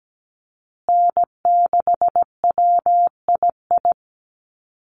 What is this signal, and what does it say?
Morse code, heard as a keyed tone, about 17 wpm, 700 Hz. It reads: N6WII
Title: USA Call Signs Repeated 3x 17wpm